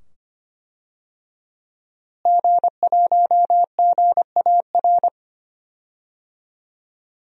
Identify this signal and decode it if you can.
Morse code, heard as a keyed tone, about 25 wpm, 700 Hz. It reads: G1GAR
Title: DXCC-100 Call Signs - 5 Characters - Repeated 2x - 25wpm